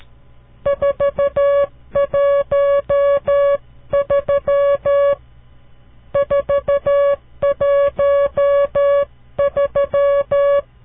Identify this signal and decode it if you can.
Morse code, heard as a keyed tone, about 13 wpm, 560 Hz. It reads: 413 413
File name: CW Learning Programs / farnsworth / unit 5